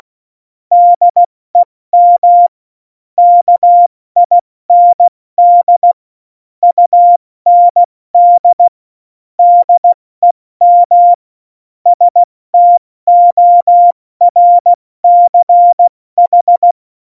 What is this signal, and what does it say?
Morse code, heard as a keyed tone, about 16 wpm, spaced as for 14 wpm, 700 Hz. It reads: DEM KIND UND DEM STORCH
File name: Morsekurs / Witze / Witz 25. Juni 2019, 70 F80